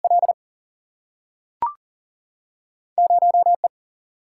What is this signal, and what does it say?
Morse code, heard as a keyed tone, about 40 wpm, 700 Hz. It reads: L 0 E
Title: Single-Letter Single-Number 40wpm